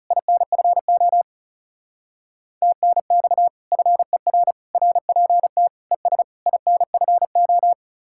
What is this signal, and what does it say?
Morse code, heard as a keyed tone, about 36 wpm, 700 Hz. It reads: INFO TNX FER RPT ES INFO
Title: Speed-Racing Common QSO Phrases Encore - 30wpm